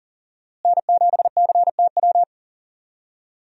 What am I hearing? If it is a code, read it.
Morse code, heard as a keyed tone, about 40 wpm, 700 Hz. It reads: N7CTW
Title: USA Call Signs - 5 Characters - Repeated 3x - 40wpm